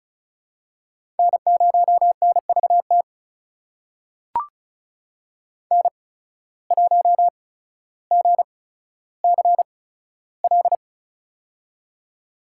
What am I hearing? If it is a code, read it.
Morse code, heard as a keyed tone, about 36 wpm, spaced as for 15 wpm, 700 Hz. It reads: N0NVT N1GCL
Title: USA Call Signs - 8x Character Spacing - 35wpm